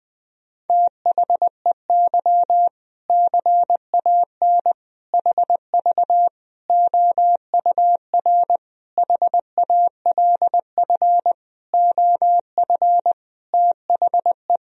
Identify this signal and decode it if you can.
Morse code, heard as a keyed tone, about 20 wpm, 700 Hz. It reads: THEY CAN HV OUR HALF OF THE
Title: Sentences from Top 400 Words and Abbreviations 20wpm